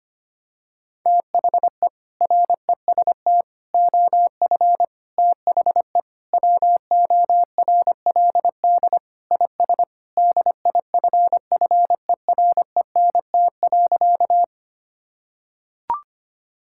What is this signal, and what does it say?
Morse code, heard as a keyed tone, about 25 wpm, 700 Hz. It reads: THE REST OF THE WORLD IS DIFFERENT.
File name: practice / 25wpm / Sentences with 4-Word Phrases - 25wpm